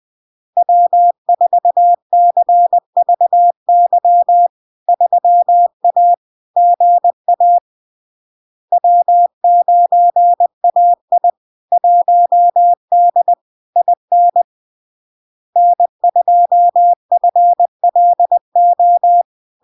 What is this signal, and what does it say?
Morse code, heard as a keyed tone, about 20 wpm, 700 Hz. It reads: W4CVY 3A GA W9AI 1D IN N2FLO
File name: LARK-FD-2022-Log-in-CW-20wpm-700-Hz